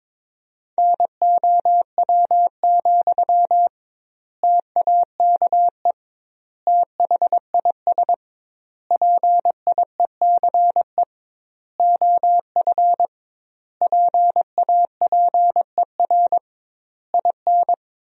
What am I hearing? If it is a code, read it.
Morse code, heard as a keyed tone, about 22 wpm, 700 Hz. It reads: NOW, TAKE THIS PIECE OF PAPER IN
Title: 2x Word-Spacing - Sentences from Top 400 Words - 22wpm